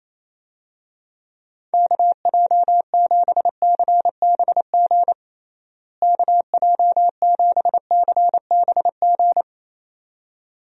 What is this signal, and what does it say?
Morse code, heard as a keyed tone, about 28 wpm, 700 Hz. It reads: KJ7CBG KJ7CBG
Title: USA Call Signs - 6 Characters - Repeated 2x - 28wpm